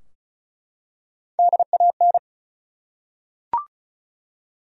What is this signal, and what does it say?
Morse code, heard as a keyed tone, about 36 wpm, 700 Hz. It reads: DAN
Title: USA Common Names - Max 5 Letters - 35wpm